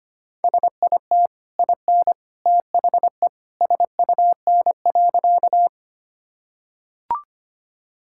Morse code, heard as a keyed tone, about 25 wpm, 700 Hz. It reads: SIT IN THE SUN.
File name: Easy Sentences with Hints - 25wpm